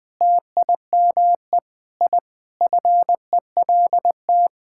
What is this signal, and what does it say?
Morse code, heard as a keyed tone, about 20 wpm, 700 Hz. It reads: TIME I FELT
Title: Sentences with 8-Word Phrases - Spoken Beginning - 20wpm